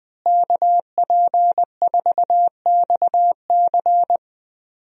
Morse code, heard as a keyed tone, about 20 wpm, 700 Hz. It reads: KP4XC
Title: DXCC-100 Call Signs - 5 Characters - Repeated 3x - 20wpm